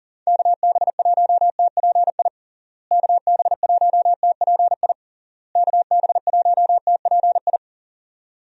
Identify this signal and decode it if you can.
Morse code, heard as a keyed tone, about 40 wpm, 700 Hz. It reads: KB1TPI KB1TPI KB1TPI
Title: USA Call Signs - 6 Characters - Repeated 3x - 40wpm